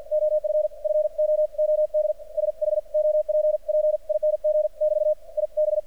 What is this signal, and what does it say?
Morse code, heard as a keyed tone, about 46 wpm, 610 Hz. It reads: OF FOOD SHOPPING <BT> I B
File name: archived / CRSnet